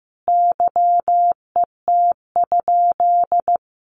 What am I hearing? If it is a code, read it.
Morse code, heard as a keyed tone, about 14 wpm, 700 Hz. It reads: YET?
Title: Sentences from Top 1500 Words 15wpm